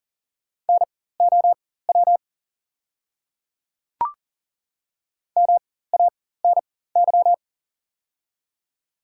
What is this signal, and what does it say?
Morse code, heard as a keyed tone, about 40 wpm, spaced as for 19 wpm, 690 Hz. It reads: NOW MANY
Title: 4x Character Spacing - Top 500 Words - 40wpm